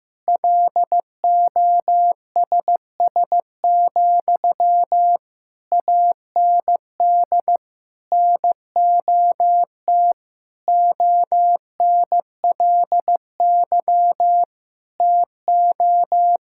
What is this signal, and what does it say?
Morse code, heard as a keyed tone, about 15 wpm, 700 Hz. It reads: LOSS, AND NOT ONLY TO
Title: Sentences from Top 1600 Words 15wpm